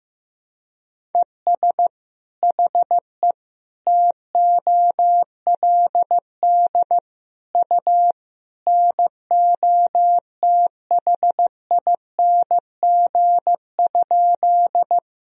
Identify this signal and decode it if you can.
Morse code, heard as a keyed tone, about 15 wpm, 700 Hz. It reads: ES HE TOLD U NOTHING?
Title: Sentences from Top 500 Words and Abbreviations 15wpm